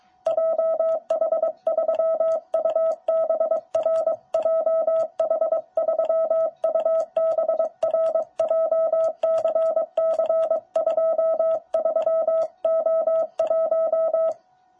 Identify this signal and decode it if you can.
Morse code, heard as a keyed tone, about 23 wpm, 650 Hz. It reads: JH3UBR JH3UBR JCC23O1
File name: jh8cbh / 2021cw